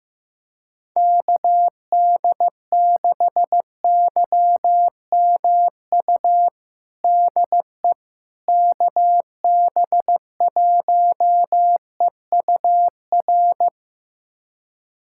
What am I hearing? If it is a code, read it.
Morse code, heard as a keyed tone, about 16 wpm, 700 Hz. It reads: KD6YMU DE KB1EUR
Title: USA Callsign DE Callsign - 6 Characters - 15wpm